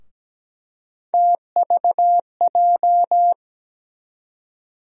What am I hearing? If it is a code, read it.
Morse code, heard as a keyed tone, about 17 wpm, 700 Hz. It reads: TVJ
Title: USA Call Sign Postfixes - 3 Letters 17wpm